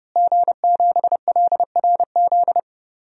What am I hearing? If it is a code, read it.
Morse code, heard as a keyed tone, about 30 wpm, 700 Hz. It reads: G7LRZ
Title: DXCC-100 Call Signs - 5 Characters - Repeated 2x - 30wpm